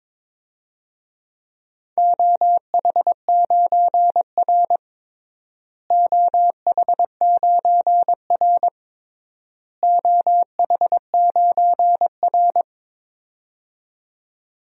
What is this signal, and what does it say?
Morse code, heard as a keyed tone, about 22 wpm, 700 Hz. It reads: OH9R OH9R OH9R
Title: DXCC-100 Call Signs - 4 Characters - Repeated 3x - 22wpm